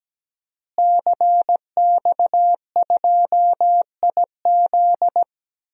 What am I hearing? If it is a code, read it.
Morse code, heard as a keyed tone, about 17 wpm, 700 Hz. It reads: CX2IZ
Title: DXCC-100 Call Signs - 5 Characters - Repeated 2x - 17wpm